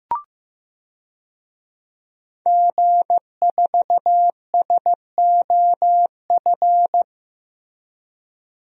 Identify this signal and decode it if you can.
Morse code, heard as a keyed tone, about 15 wpm, 700 Hz. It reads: G4SOF
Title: DXCC-100 Call Signs - 5 Characters - Repeated 2x - 15wpm